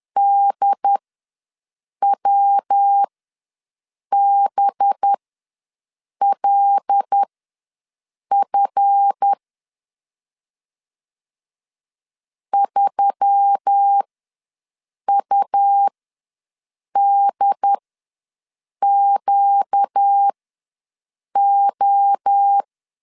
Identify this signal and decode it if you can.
Morse code, heard as a keyed tone, about 10 wpm, spaced as for 6 wpm, 800 Hz. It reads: DWBLF 3UDQO
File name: LEZIONI / 7 GF9Z / 7 GF ws 10wpm 3918